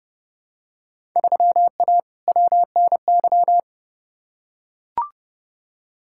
Morse code, heard as a keyed tone, about 30 wpm, 700 Hz. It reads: 3A WNY
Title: Field Day Exchanges - 30wpm